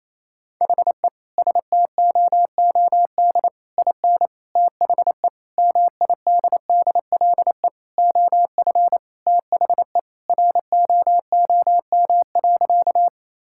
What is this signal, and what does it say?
Morse code, heard as a keyed tone, about 28 wpm, 700 Hz. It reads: HE STOOD IN THE MIDDLE OF THE ROOM.
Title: Sentences from Top 1000 Words - Review Part 6 - 28wpm